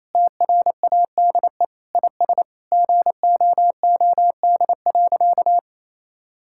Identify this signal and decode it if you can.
Morse code, heard as a keyed tone, about 28 wpm, 700 Hz. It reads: TRADE IS GOOD.
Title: Sentences from Top 1000 Words - Review Part 6 - 28wpm